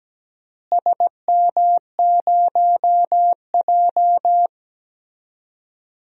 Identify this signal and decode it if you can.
Morse code, heard as a keyed tone, about 17 wpm, 700 Hz. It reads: SM0J
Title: DXCC-100 Call Signs - 4 Characters - Repeated 3x - 17wpm